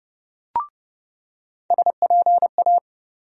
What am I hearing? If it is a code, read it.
Morse code, heard as a keyed tone, about 30 wpm, 700 Hz. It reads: SPA
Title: USA Call Sign Postfixes - 3 Letters 30wpm